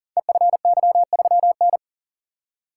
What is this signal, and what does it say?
Morse code, heard as a keyed tone, about 39 wpm, 690 Hz. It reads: EFY3N
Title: Code Groups - 5 Letter:Numbers - 40wpm